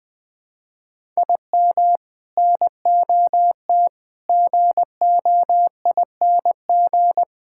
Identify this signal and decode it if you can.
Morse code, heard as a keyed tone, about 20 wpm, 700 Hz. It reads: IM NOT GOING
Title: Sentences with 3-Word Phrases - 20wpm